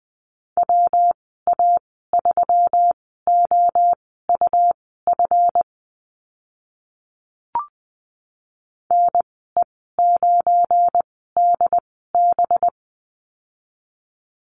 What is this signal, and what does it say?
Morse code, heard as a keyed tone, about 20 wpm, spaced as for 14 wpm, 700 Hz. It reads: WA3OUF NE9DB
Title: 193 Lesson 31g - Letter v - Call Signs 20wpm